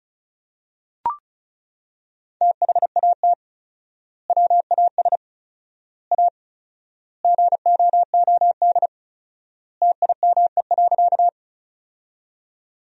Morse code, heard as a keyed tone, about 34 wpm, 700 Hz. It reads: THAT WAS A GOOD TIME.
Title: 4x Word-Spacing - Sentences from Top 200 Words - 35wpm